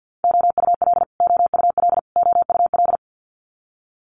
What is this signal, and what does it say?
Morse code, heard as a keyed tone, about 50 wpm, 700 Hz. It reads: OUL OUL OUL
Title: Multi-letter Phonemes Repeated 3x 50wpm